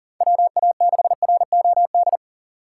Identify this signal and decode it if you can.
Morse code, heard as a keyed tone, about 40 wpm, 700 Hz. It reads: WA6ROD
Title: USA Call Signs - 6 Characters - 2x Character Spacing - 40wpm